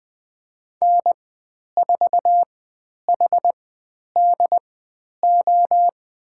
Morse code, heard as a keyed tone, about 20 wpm, spaced as for 10 wpm, 700 Hz. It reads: N4HDO
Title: 102 Lesson 17g - Letter u - Call Signs 20wpm